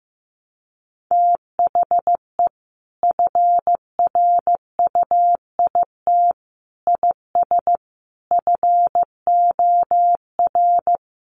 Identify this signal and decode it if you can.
Morse code, heard as a keyed tone, about 15 wpm, 700 Hz. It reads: THE FRUIT IS FOR
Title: Sentences from Top 1300 Words 15wpm